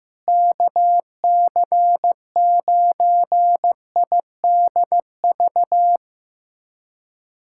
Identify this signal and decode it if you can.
Morse code, heard as a keyed tone, about 15 wpm, 700 Hz. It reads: KC9IDV
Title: USA Callsign DE Callsign - 6 Characters - 15wpm